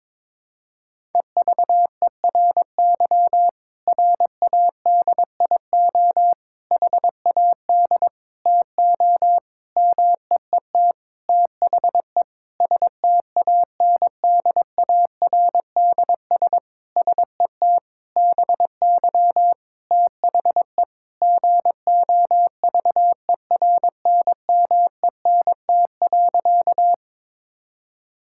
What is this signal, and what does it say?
Morse code, heard as a keyed tone, about 22 wpm, 700 Hz. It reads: EVERY RADIO HAD TO MEET THE STANDARDS SET BY THE GOVERNMENT.
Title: Sentences from Top 1000 Words - Review Part 6 - 22wpm